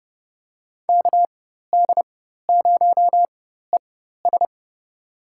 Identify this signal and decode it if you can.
Morse code, heard as a keyed tone, about 30 wpm, spaced as for 14 wpm, 700 Hz. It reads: KD0ES
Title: USA Call Signs - 4x Character Spacing - 30wpm